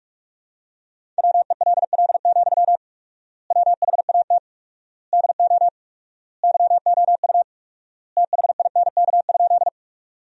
Morse code, heard as a keyed tone, about 46 wpm, 700 Hz. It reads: WELL, WHAT DO YOU THINK?
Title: 4x Word-Spacing - Sentences from Top 200 Words - 45wpm